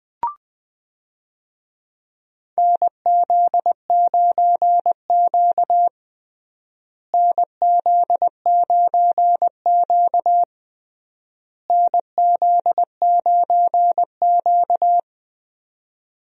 Morse code, heard as a keyed tone, about 20 wpm, 700 Hz. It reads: NZ9Q NZ9Q NZ9Q
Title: USA Call Signs - 4 Characters - Repeated 3x - 20wpm